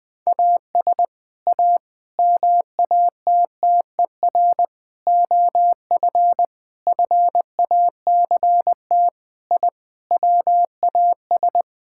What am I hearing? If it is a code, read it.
Morse code, heard as a keyed tone, about 21 wpm, 700 Hz. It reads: AS A MATTER OF FACT I WAS
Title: Sentences with 5-Word Phrases - 20wpm